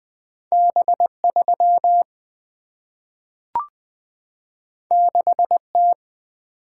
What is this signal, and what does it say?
Morse code, heard as a keyed tone, about 20 wpm, 700 Hz. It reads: B 3 6 T
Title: Single-Letter Single-Number 20wpm